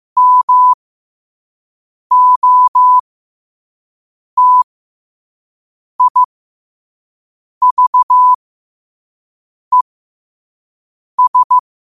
Morse code, heard as a keyed tone, about 15 wpm, spaced as for 5 wpm, 1000 Hz. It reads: MOTIVES
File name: qotdfiles / 5wpm